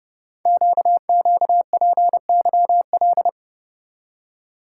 Morse code, heard as a keyed tone, about 30 wpm, 700 Hz. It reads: QQPYL
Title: Code Groups - 5 Letters - 30wpm